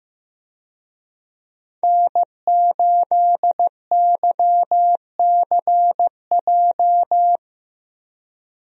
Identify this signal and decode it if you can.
Morse code, heard as a keyed tone, about 15 wpm, 700 Hz. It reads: N8YCJ
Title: USA Call Signs - 5 Characters - Repeated 3x - 15wpm